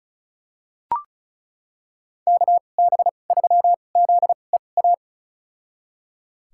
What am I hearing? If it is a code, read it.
Morse code, heard as a keyed tone, about 36 wpm, spaced as for 25 wpm, 700 Hz. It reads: KB3ZEA
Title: USA Call Signs - 2x Character Spacing - 35wpm